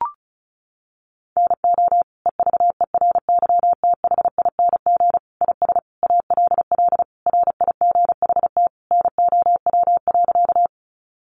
Morse code, heard as a keyed tone, about 34 wpm, 700 Hz. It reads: NO EVERYTHING IS ALL RIGHT NOW.
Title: Sentences from Top 700 Words 35wpm